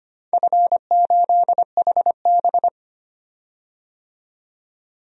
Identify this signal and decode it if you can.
Morse code, heard as a keyed tone, about 25 wpm, 700 Hz. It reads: F8HB
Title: DXCC-100 Call Signs - 4 Characters - 25wpm